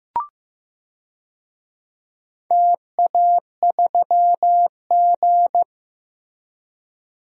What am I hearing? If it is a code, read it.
Morse code, heard as a keyed tone, about 15 wpm, 700 Hz. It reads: TA3G
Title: DXCC-100 Call Signs - 4 Characters - Repeated 3x - 15wpm